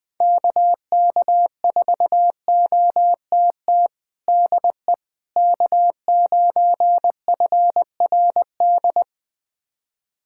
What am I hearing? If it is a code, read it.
Morse code, heard as a keyed tone, about 20 wpm, 700 Hz. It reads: KK4OTT DE K9FRD
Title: USA Callsign DE Callsign 20wpm